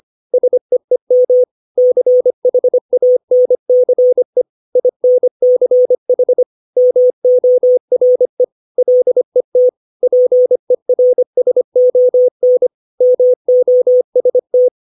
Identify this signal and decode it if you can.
Morse code, heard as a keyed tone, about 25 wpm, 500 Hz. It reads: SEEM CHANCE INCH MORE LET PERSON MOST
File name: 632 words / 632 Words 25WPM